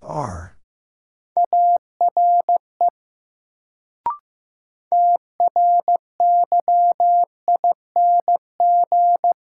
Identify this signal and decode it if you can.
Morse code, heard as a keyed tone, about 15 wpm, 700 Hz. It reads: ARE TRYING
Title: Sets of 2 Words - New Word Drill - 15wpm